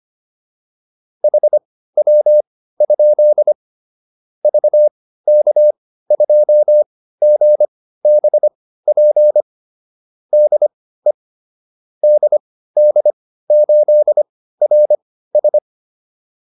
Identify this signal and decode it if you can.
Morse code, heard as a keyed tone, about 25 wpm, spaced as for 15 wpm, 600 Hz. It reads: HW? VK2GBP DE DD8RS
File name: QSO 405 15